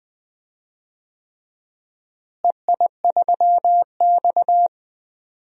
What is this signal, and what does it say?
Morse code, heard as a keyed tone, about 20 wpm, 700 Hz. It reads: EI3X
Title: DXCC-100 Call Signs - 4 Characters - Repeated 3x - 20wpm